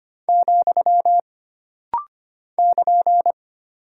Morse code, heard as a keyed tone, about 25 wpm, 700 Hz. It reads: , <KN>
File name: Single Punctuation-Prosign - Rapid-Fire - 25wpm